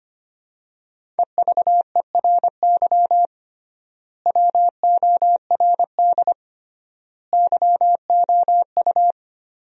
Morse code, heard as a keyed tone, about 25 wpm, 700 Hz. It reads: EVERY WORD YOU
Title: 3x Word-Spacing - Sentences from Top 400 Words - 25wpm